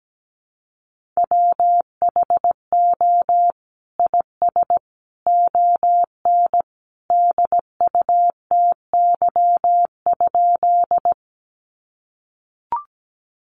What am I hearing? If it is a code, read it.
Morse code, heard as a keyed tone, about 17 wpm, 700 Hz. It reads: WHO IS ON DUTY?
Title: Sentences from Top 1500 Words 17wpm